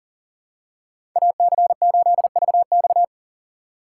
Morse code, heard as a keyed tone, about 40 wpm, 700 Hz. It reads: AC8VX
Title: USA Call Signs - 5 Characters - 3x Character Spacing - 40wpm